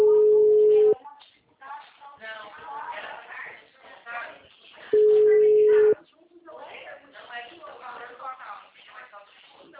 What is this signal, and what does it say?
Ringback tone (Europe, 425 Hz), with speech.
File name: out- -2030- - - .66440